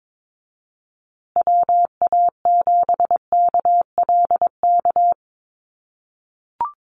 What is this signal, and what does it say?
Morse code, heard as a keyed tone, about 22 wpm, 700 Hz. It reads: WA7KLK
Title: USA Call Signs Repeated 3x 22wpm